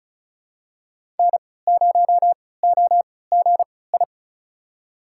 Morse code, heard as a keyed tone, about 36 wpm, spaced as for 20 wpm, 700 Hz. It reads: N0OGI
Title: USA Call Signs - 5 Characters - 3x Character Spacing - 35wpm